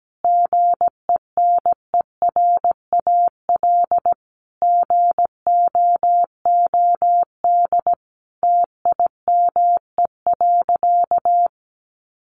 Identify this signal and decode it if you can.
Morse code, heard as a keyed tone, about 17 wpm, 700 Hz. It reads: GENERAL GOOD TIME.
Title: Sentences from Top 600 Words 17wpm